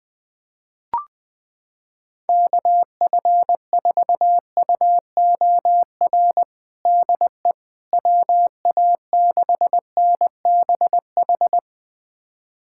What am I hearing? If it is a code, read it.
Morse code, heard as a keyed tone, about 20 wpm, 700 Hz. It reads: KF4UOR DE WA6NBH
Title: USA Callsign DE Callsign - 6 Characters - 20wpm